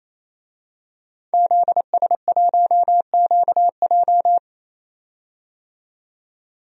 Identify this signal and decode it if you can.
Morse code, heard as a keyed tone, about 27 wpm, 700 Hz. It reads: ZS1QJ
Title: DXCC-100 Call Signs - 5 Characters - 28wpm